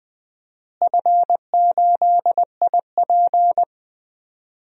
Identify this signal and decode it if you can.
Morse code, heard as a keyed tone, about 20 wpm, 700 Hz. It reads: F8IP
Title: DXCC-100 Call Signs - 4 Characters - Repeated 2x - 20wpm